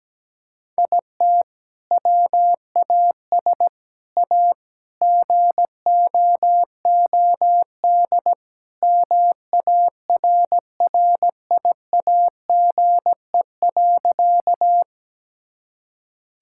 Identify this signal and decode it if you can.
Morse code, heard as a keyed tone, about 17 wpm, 700 Hz. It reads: IT WAS A GOOD MARRIAGE.
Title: Sentences from Top 1000 Words - Review Part 5 - 17wpm